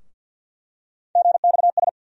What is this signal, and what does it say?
Morse code, heard as a keyed tone, about 52 wpm, 700 Hz. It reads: GXS
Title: USA Call Sign Postfixes - 3 Letters 50wpm